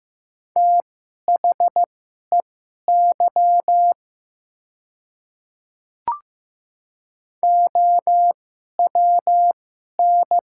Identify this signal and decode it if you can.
Morse code, heard as a keyed tone, about 15 wpm, spaced as for 11 wpm, 700 Hz. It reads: THEY OWN
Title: 2x Character Spacing - Top 300 Words - 15wpm